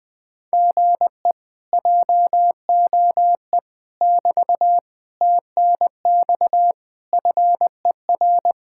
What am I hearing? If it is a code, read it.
Morse code, heard as a keyed tone, about 20 wpm, 700 Hz. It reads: GE JOE <BT> TNX FER